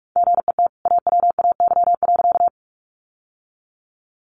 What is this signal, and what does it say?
Morse code, heard as a keyed tone, about 44 wpm, 700 Hz. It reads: GET AWAY.
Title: Sentences from Top 600 Words 45wpm